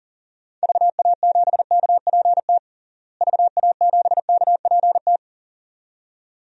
Morse code, heard as a keyed tone, about 37 wpm, 700 Hz. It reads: VA7KPT VA7KPT
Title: DXCC-100 Call Signs - 6 Characters - Repeated 2x - 40wpm